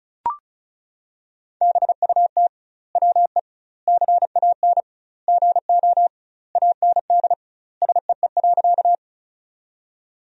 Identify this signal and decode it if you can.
Morse code, heard as a keyed tone, about 34 wpm, 700 Hz. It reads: BUT WE CAN GO AND SEE.
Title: 2x Word-Spacing - Sentences from Top 100 Words - 35wpm